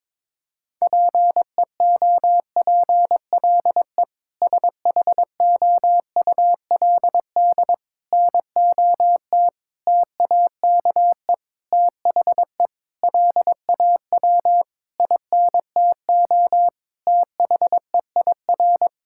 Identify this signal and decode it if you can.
Morse code, heard as a keyed tone, about 22 wpm, 700 Hz. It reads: PEOPLE SHOULD NOT TAKE THE LAW INTO THEIR
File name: practice / 22wpm / Sentences with 8-Word Phrases - 22wpm